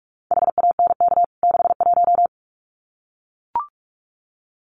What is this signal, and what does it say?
Morse code, heard as a keyed tone, about 44 wpm, 700 Hz. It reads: HANK 61
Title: CWT Exchange 45wpm